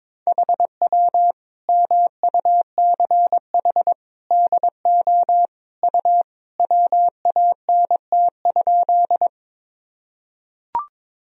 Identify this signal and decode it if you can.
Morse code, heard as a keyed tone, about 22 wpm, 700 Hz. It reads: HW MUCH DO U WANT?
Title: Sentences from Top 200 Words and Abbreviations 22wpm